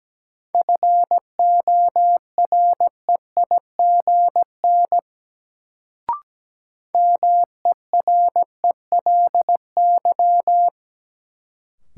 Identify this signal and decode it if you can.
Morse code, heard as a keyed tone, about 17 wpm, 700 Hz. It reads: FOREIGN MERELY
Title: Sentences from Top 1400 Words - New Word Drill - 17wpm